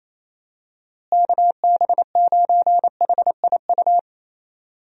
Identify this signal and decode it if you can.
Morse code, heard as a keyed tone, about 28 wpm, 700 Hz. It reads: KB9HIU
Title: USA Call Signs - 6 Characters - Repeated 3x - 28wpm